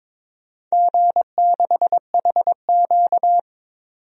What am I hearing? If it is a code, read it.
Morse code, heard as a keyed tone, about 23 wpm, 700 Hz. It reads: G6HQ
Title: DXCC-100 Call Signs - 4 Characters - 22wpm